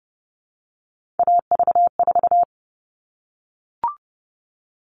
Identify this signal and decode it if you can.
Morse code, heard as a keyed tone, about 30 wpm, 700 Hz. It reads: AV4
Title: All Cut Numbers - 3-Digit 30wpm